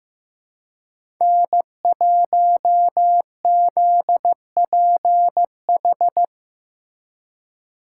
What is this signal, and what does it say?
Morse code, heard as a keyed tone, about 18 wpm, 700 Hz. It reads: N1ZPH
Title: Speed Racing - USA 1x3 Call Signs 15wpm